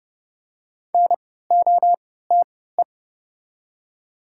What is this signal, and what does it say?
Morse code, heard as a keyed tone, about 30 wpm, spaced as for 17 wpm, 690 Hz. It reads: NOTE
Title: 3x Character Spacing - Top 500 Words - 30wpm